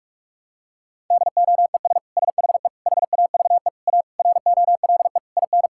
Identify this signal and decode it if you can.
Morse code, heard as a keyed tone, about 46 wpm, 700 Hz. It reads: DOES SHE HAVE A ROLE IN
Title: Sentences from Top 1000 Words - Review Part 6 - 45wpm